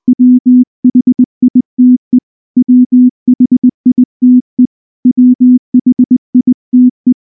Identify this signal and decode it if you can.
Morse code, heard as a keyed tone, about 19 wpm, 250 Hz. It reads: WHITE WHITE WHITE
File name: morse / morse-training-20WPM-10